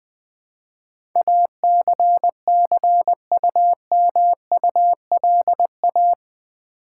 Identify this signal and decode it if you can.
Morse code, heard as a keyed tone, about 20 wpm, 700 Hz. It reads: ACCUMULA
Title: Build A Word - 4 Letters At A Time - 10-Letter Words - 20wpm